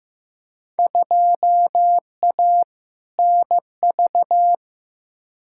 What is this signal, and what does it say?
Morse code, heard as a keyed tone, about 15 wpm, 700 Hz. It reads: 2A NV
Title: Field Day Exchanges - 15wpm